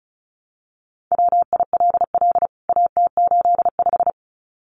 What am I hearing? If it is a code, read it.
Morse code, heard as a keyed tone, about 34 wpm, 700 Hz. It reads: WILL AT85
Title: CWT Exchange 35wpm